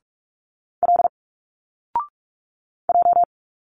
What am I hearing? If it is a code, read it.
Morse code, heard as a keyed tone, about 44 wpm, 700 Hz. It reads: L J
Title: Single Letters 45wpm